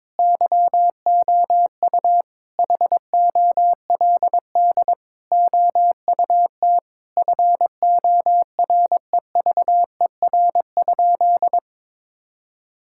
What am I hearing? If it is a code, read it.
Morse code, heard as a keyed tone, about 22 wpm, 700 Hz. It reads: YOU HOLD OUT FOREVER?
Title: Sentences from Top 1600 Words 22wpm